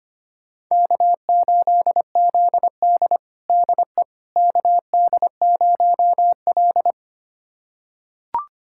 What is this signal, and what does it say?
Morse code, heard as a keyed tone, about 25 wpm, 700 Hz. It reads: K8ZD DE KD0L
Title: USA Callsign DE Callsign - 4 Characters - 25wpm